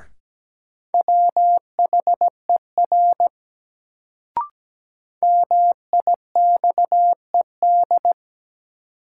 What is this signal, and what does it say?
Morse code, heard as a keyed tone, about 17 wpm, 700 Hz. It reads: WHER MIXED
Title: Sentences from Top 1500 Words - New Word Drill - 17wpm